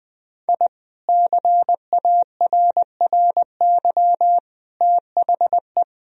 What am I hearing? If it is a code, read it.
Morse code, heard as a keyed tone, about 20 wpm, 700 Hz. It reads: I CARRY THE
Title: Sentences from Top 1000 Words - Review Part 1 - 20wpm